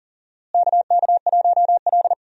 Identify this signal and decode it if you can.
Morse code, heard as a keyed tone, about 40 wpm, 700 Hz. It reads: KK1L
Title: USA Call Signs - 4 Characters - Repeated 2x - 40wpm